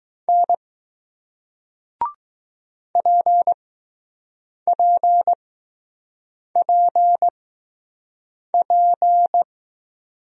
Morse code, heard as a keyed tone, about 20 wpm, spaced as for 6 wpm, 700 Hz. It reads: N PPPP